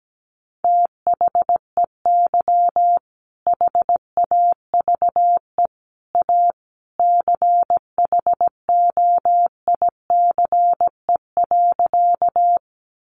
Morse code, heard as a keyed tone, about 17 wpm, 700 Hz. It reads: THEY HAVE A CHOICE.
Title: Sentences from Top 900 Words 17wpm